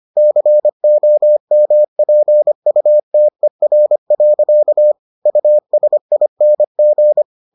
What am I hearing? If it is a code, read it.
Morse code, heard as a keyed tone, about 25 wpm, 600 Hz. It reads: COMPUTER. USING